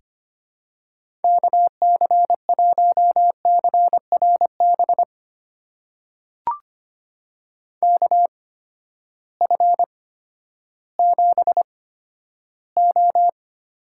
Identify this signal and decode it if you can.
Morse code, heard as a keyed tone, about 25 wpm, spaced as for 12 wpm, 700 Hz. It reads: KC1CRB KF7O
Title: USA Call Signs - 6 Characters - 8x Character Spacing - 25wpm